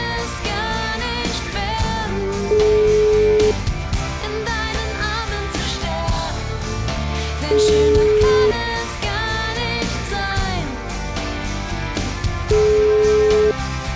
Ringback tone (Europe, 425 Hz), with noise.